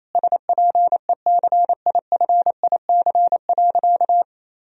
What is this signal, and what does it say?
Morse code, heard as a keyed tone, about 28 wpm, 700 Hz. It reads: SPECIFIC.
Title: Sentences from Top 1000 Words - Review Part 3 - 28wpm